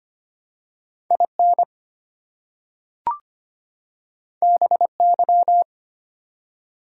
Morse code, heard as a keyed tone, about 25 wpm, 700 Hz. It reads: IN BY
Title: Two Letter Words - 25wpm